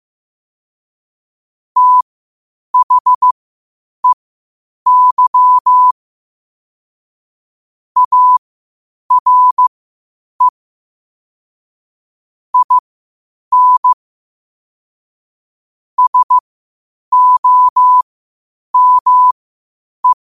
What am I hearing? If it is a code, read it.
Morse code, heard as a keyed tone, about 15 wpm, spaced as for 8 wpm, 1000 Hz. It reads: THEY ARE IN SOME
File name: qotdfiles / 8wpm